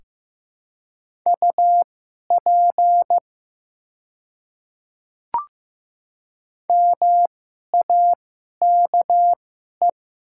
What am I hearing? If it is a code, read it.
Morse code, heard as a keyed tone, about 15 wpm, spaced as for 11 wpm, 700 Hz. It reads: UP MAKE
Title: 2x Character Spacing - Top 100 Words - 15wpm